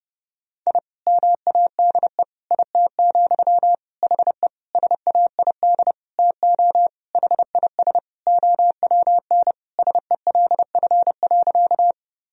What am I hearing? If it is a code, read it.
Morse code, heard as a keyed tone, about 30 wpm, 700 Hz. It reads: I MADE IT, HE SAID TO HIS OWN SELF.
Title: Sentences from Top 1000 Words - Review Part 1 - 30wpm